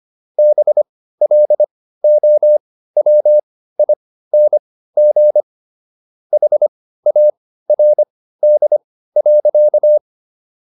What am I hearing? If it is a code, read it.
Morse code, heard as a keyed tone, about 25 wpm, spaced as for 15 wpm, 600 Hz. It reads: BLOWING HARD.